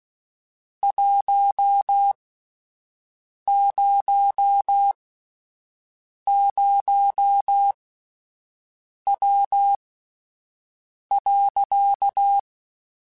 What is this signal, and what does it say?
Morse code, heard as a keyed tone, about 16 wpm, spaced as for 5 wpm, 800 Hz. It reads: 100W.
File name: (spaced) / QSOs Spaced-mp3s / WD0FBT 1 @10 sp